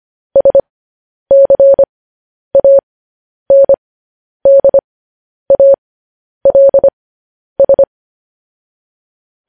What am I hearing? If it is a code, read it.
Morse code, heard as a keyed tone, about 25 wpm, spaced as for 10 wpm, 550 Hz. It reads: SCANDALS
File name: podcast / News headlines-1x-25x10